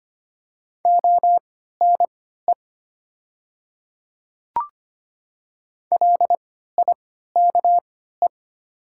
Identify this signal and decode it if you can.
Morse code, heard as a keyed tone, about 25 wpm, spaced as for 14 wpm, 700 Hz. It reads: ONE LIKE
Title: 3x Character Spacing - Top 200 Words - 25wpm